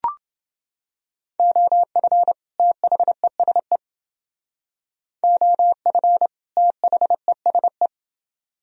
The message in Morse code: OF THESE OF THESE

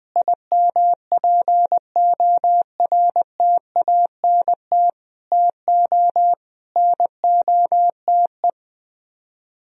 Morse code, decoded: IMPORTANT TO NOTE

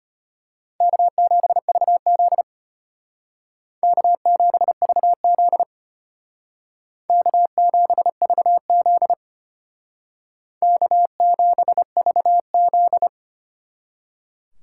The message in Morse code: K7VZ K7VZ K7VZ K7VZ

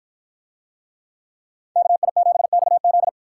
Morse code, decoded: KI7XZ

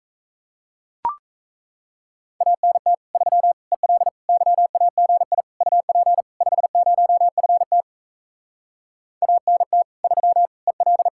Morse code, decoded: ANT 3 EL YAGI UP 50FT ANT 3 EL

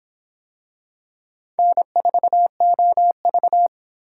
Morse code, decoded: N4OV

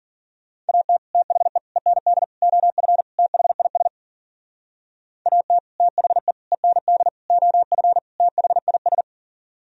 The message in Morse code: AT THE END OF THIS AT THE END OF THIS

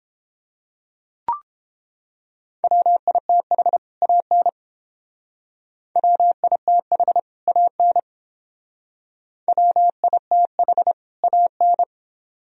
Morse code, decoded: WITH AN WITH AN WITH AN

